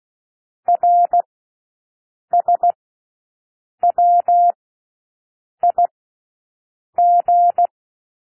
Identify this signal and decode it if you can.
Morse code, heard as a keyed tone, about 16 wpm, spaced as for 6 wpm, 700 Hz. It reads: RSWIG
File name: cwmp3 / L1 SOUND LETTERS 16 wpm